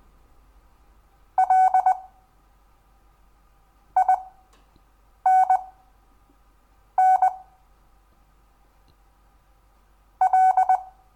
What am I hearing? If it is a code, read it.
Morse code, heard as a keyed tone, about 20 wpm, spaced as for 5 wpm, 750 Hz. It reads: LINN L
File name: cw / Lesson10 REVIEW